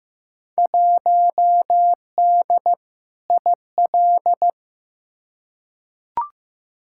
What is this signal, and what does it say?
Morse code, heard as a keyed tone, about 15 wpm, 700 Hz. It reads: 1D IL
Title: Field Day Exchanges - 15wpm